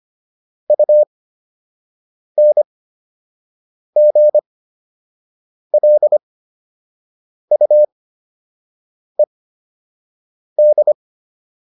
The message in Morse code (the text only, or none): UNGLUED